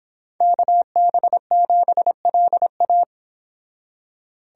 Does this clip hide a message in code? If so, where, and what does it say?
Yes: Morse code from 0.4 to 3.0 s, heard as a keyed tone, about 26 wpm, 700 Hz. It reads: KB7LA